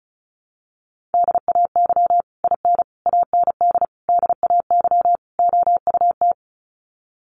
Morse code, heard as a keyed tone, about 34 wpm, 700 Hz. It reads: DAY IN AND DAY OUT
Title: Sets of 5 Words Encore2 35wpm